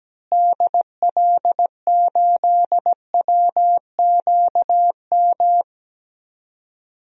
Morse code, heard as a keyed tone, about 17 wpm, 700 Hz. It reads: DL8WQM
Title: DXCC-100 Call Signs - 6 Characters - Repeated 2x - 17wpm